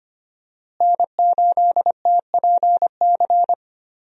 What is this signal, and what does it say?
Morse code, heard as a keyed tone, about 25 wpm, 700 Hz. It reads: N8TPC
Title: USA Call Signs - 5 Characters - Repeated 2x - 25wpm